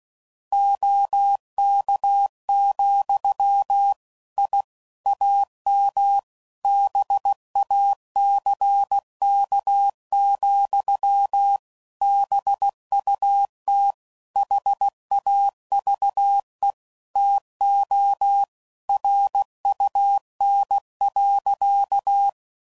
Morse code, heard as a keyed tone, about 16 wpm, 800 Hz. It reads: OK, I AM BACK, BUT HAVE TO RUN.
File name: CW Code Groups @15WPM / QSOs / kc7sac